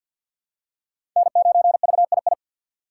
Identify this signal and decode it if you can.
Morse code, heard as a keyed tone, about 46 wpm, 700 Hz. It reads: N9VII
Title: USA Call Signs - 5 Characters - 6x Character Spacing - 50wpm